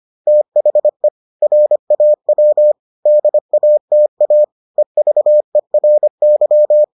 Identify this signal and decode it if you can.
Morse code, heard as a keyed tone, about 25 wpm, 600 Hz. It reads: THE RAW DATA EVERY